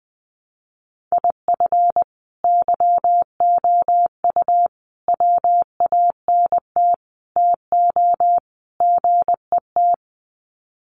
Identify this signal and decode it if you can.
Morse code, heard as a keyed tone, about 20 wpm, 700 Hz. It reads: IF YOU WANT TO GET
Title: Sets of 5 Words Encore 20wpm